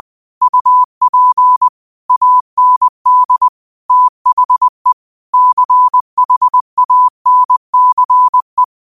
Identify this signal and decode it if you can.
Morse code, heard as a keyed tone, about 20 wpm, 1000 Hz. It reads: UP AND THE CHANCE